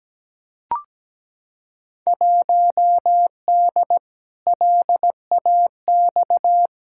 Morse code, heard as a keyed tone, about 17 wpm, 700 Hz. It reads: 1D LAX
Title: Field Day Exchanges - 17wpm